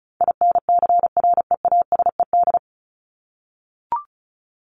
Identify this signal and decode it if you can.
Morse code, heard as a keyed tone, about 34 wpm, 700 Hz. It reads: INCREASED
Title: ED Suffix - Top 100 Words with Base Words 35wpm